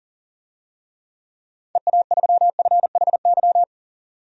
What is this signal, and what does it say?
Morse code, heard as a keyed tone, about 40 wpm, 700 Hz. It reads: EA3FHY
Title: DXCC-100 Call Signs - 6 Characters - 40wpm